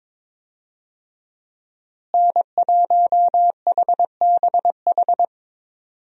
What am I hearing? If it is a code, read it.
Morse code, heard as a keyed tone, about 22 wpm, 700 Hz. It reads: N1HBH
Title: USA Call Signs - 5 Characters - Repeated 2x - 22wpm